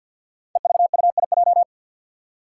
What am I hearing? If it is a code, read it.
Morse code, heard as a keyed tone, about 47 wpm, 700 Hz. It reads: EVUIJ
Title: Five Challenging Letter-Number Combinations 50wpm